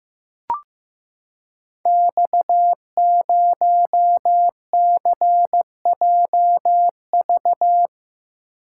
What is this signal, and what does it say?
Morse code, heard as a keyed tone, about 15 wpm, 700 Hz. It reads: X0CJV
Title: Code Groups - 5 Letter:Numbers - 15wpm